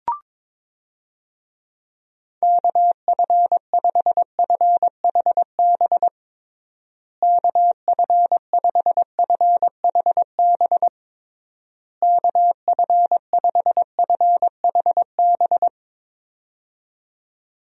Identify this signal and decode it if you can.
Morse code, heard as a keyed tone, about 22 wpm, 700 Hz. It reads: KF5FHB KF5FHB KF5FHB
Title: USA Call Signs - 6 Characters - Repeated 3x - 22wpm